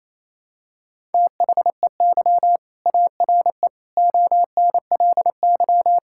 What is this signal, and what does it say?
Morse code, heard as a keyed tone, about 28 wpm, 700 Hz. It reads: THEY ARE ONLY
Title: Sentences from Top 1000 Words - Review Part 6 - 28wpm